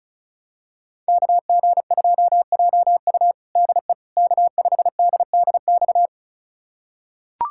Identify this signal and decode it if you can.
Morse code, heard as a keyed tone, about 36 wpm, 700 Hz. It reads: KG2JU DE K5DDX
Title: USA Callsign DE Callsign - 5 Characters - 35wpm